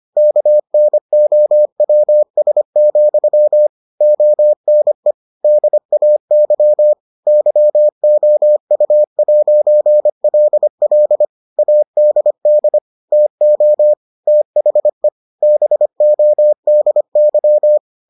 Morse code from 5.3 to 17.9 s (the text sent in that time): DAY YOU'LL ADD TO THE BODY